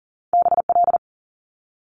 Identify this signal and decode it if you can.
Morse code, heard as a keyed tone, about 40 wpm, 700 Hz. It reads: B L
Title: Top Two Letter Combinations in Words 40wpm